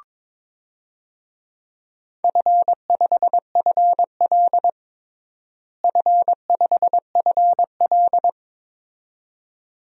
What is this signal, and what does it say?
Morse code, heard as a keyed tone, about 22 wpm, 700 Hz. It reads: F5FL F5FL
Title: DXCC-100 Call Signs - 4 Characters - Repeated 2x - 22wpm